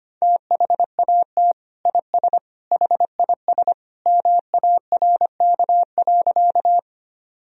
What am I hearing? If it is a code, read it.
Morse code, heard as a keyed tone, about 25 wpm, 700 Hz. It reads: THAT IS HIS MARK.
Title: Sentences from Top 1000 Words - Review Part 1 - 25wpm